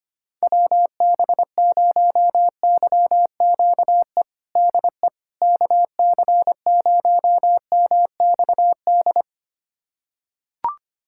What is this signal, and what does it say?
Morse code, heard as a keyed tone, about 25 wpm, 700 Hz. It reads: WB0YQE DE KC0MXD
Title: USA Callsign DE Callsign - 6 Characters - 25wpm